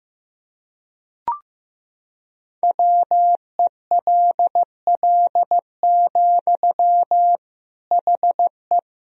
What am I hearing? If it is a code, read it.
Morse code, heard as a keyed tone, about 15 wpm, 700 Hz. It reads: WELL, HE
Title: Sentences from Top 1000 Words - Review Part 2 - 15wpm